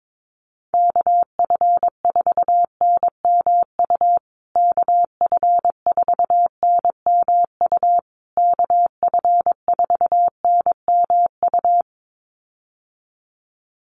Morse code, heard as a keyed tone, about 22 wpm, 700 Hz. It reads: KF4NMU KF4NMU KF4NMU
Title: USA Call Signs Repeated 3x 22wpm